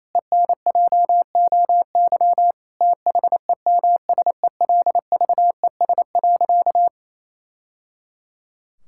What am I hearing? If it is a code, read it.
Morse code, heard as a keyed tone, about 28 wpm, 700 Hz. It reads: ENJOY THEMSELVES.
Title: Sentences from Top 1000 Words - Review Part 5 - 28wpm